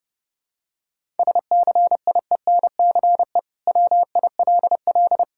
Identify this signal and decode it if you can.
Morse code, heard as a keyed tone, about 32 wpm, 700 Hz. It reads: SCIENCE WILL
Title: Sentences from Top 1000 Words - Review Part 6 - 30wpm